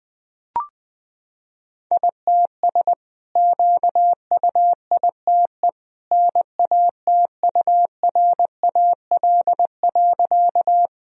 Morse code, heard as a keyed tone, about 21 wpm, 700 Hz. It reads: ITS QUITE NATURAL.
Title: Sentences from Top 1000 Words - Review Part 4 - 20wpm